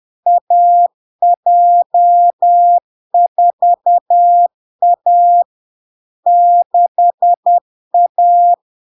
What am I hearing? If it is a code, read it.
Morse code, heard as a keyed tone, about 10 wpm, 700 Hz. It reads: AJ4A 6A
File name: LARK-FD-2022-Log-in-CW-10wpm-700-Hz